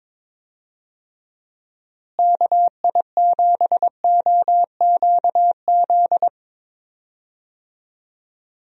Morse code, heard as a keyed tone, about 22 wpm, 700 Hz. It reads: KI7OQZ
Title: USA Call Signs - 6 Characters - Repeated 3x - 22wpm